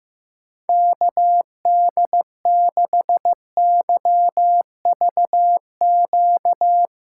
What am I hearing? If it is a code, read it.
Morse code, heard as a keyed tone, about 16 wpm, 700 Hz. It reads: KD6YVQ
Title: USA Call Signs - 6 Characters - 3x Character Spacing - 15wpm